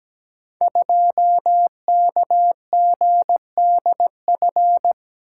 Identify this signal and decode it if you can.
Morse code, heard as a keyed tone, about 17 wpm, 700 Hz. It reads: 2KGDF
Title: Code Groups - 5 Letter:Numbers - 17wpm